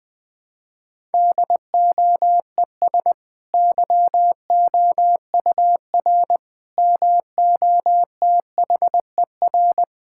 Morse code, heard as a keyed tone, about 20 wpm, 700 Hz. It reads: DOES YOUR MOTHER
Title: Sentences from Top 1000 Words - Review Part 5 - 20wpm